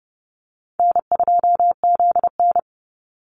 Morse code, heard as a keyed tone, about 30 wpm, 700 Hz. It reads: N2ZN
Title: Easy Call Signs - 30wpm